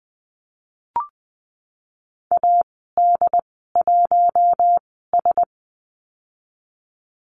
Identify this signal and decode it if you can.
Morse code, heard as a keyed tone, about 20 wpm, 700 Hz. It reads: A D 1 S
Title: 076 Lesson 13g - Letter L - Call Signs 20wpm